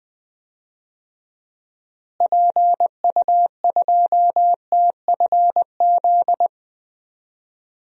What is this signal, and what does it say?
Morse code, heard as a keyed tone, about 20 wpm, 700 Hz. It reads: PU2TFZ
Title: DXCC-100 Call Signs - 6 Characters - Repeated 2x - 20wpm